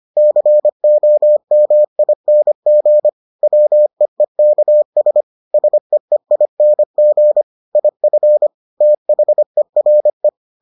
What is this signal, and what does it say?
Morse code, heard as a keyed tone, about 25 wpm, 600 Hz. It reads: COMING WEEKS SEEING IF THERE